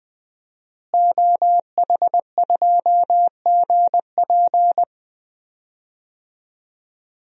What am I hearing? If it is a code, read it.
Morse code, heard as a keyed tone, about 21 wpm, 700 Hz. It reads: OH2GP
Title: DXCC-100 Call Signs - 5 Characters - Repeated 2x - 20wpm